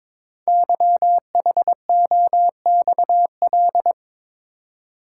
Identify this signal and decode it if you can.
Morse code, heard as a keyed tone, about 22 wpm, 700 Hz. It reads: YHOXL
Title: Code Groups - 5 Letters - 22wpm